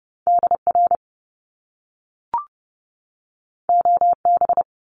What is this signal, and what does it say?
Morse code, heard as a keyed tone, about 30 wpm, 700 Hz. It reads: DR OB
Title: Top Two Letter Combinations in Words 30wpm